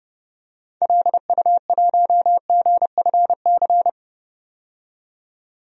Morse code, heard as a keyed tone, about 30 wpm, 700 Hz. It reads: LU1GFC
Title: DXCC-100 Call Signs - 6 Characters - 30wpm